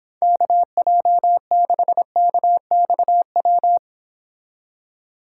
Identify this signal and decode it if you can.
Morse code, heard as a keyed tone, about 26 wpm, 700 Hz. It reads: KJ6KXW